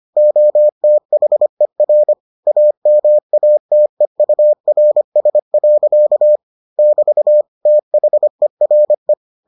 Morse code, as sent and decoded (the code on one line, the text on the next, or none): --- - .... .-. / .- -- .- - . ..- .-. ... .-.-.- / -...- / - .... .-. .
OTHER AMATEURS. <BT> THERE